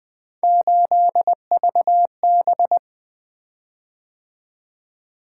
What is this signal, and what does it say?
Morse code, heard as a keyed tone, about 20 wpm, 700 Hz. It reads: 8VB
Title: Three Letter-Number Combinations - 20wpm